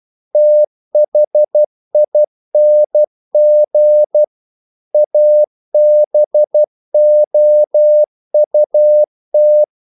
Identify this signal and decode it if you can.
Morse code, heard as a keyed tone, about 12 wpm, 600 Hz. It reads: THING ABOUT
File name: CW / Practice / EnglishQuotations0312